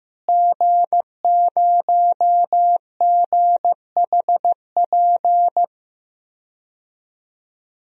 Morse code, heard as a keyed tone, about 15 wpm, 700 Hz. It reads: G0GHP
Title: DXCC-100 Call Signs - 5 Characters - Repeated 2x - 15wpm